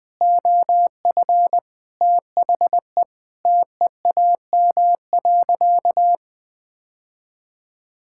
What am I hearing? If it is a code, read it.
Morse code, heard as a keyed tone, about 20 wpm, 700 Hz. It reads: OF THE TEAM.
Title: Sentences with 5-Word Phrases - 20wpm